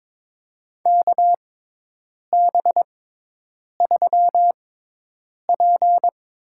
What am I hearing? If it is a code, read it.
Morse code, heard as a keyed tone, about 22 wpm, spaced as for 8 wpm, 700 Hz. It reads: KB3P
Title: USA Call Signs - 6 Characters - 6x Character Spacing - 22wpm